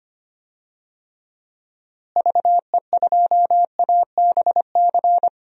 Morse code, heard as a keyed tone, about 25 wpm, 700 Hz. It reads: VE2ABC
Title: DXCC-100 Call Signs - 6 Characters - Repeated 3x - 25wpm